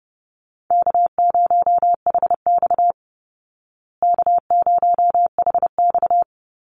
Morse code, heard as a keyed tone, about 30 wpm, 700 Hz. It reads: K0HX K0HX
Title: Easy Call Signs Repeated 3x - 30wpm